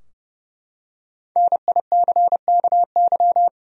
Morse code, heard as a keyed tone, about 30 wpm, 700 Hz. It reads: NICKY
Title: USA Common Names - Max 5 Letters - 30wpm